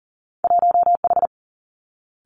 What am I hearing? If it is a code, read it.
Morse code, heard as a keyed tone, about 38 wpm, 700 Hz. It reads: 1H
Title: Two Challenging Letter-Number Combinations 40wpm